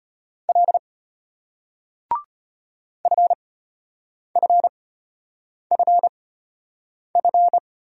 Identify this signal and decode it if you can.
Morse code, heard as a keyed tone, about 34 wpm, spaced as for 8 wpm, 700 Hz. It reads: L FFFF